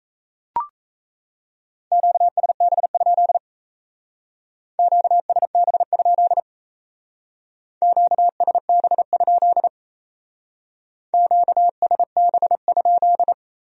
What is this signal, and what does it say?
Morse code, heard as a keyed tone, about 37 wpm, 700 Hz. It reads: QSB? QSB? QSB? QSB?